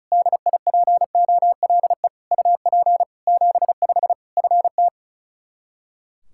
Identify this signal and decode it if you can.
Morse code, heard as a keyed tone, about 34 wpm, 700 Hz. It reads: DIPOLE UP 75 FT
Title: Practice QSO - Exchange 2 - Info - 35wpm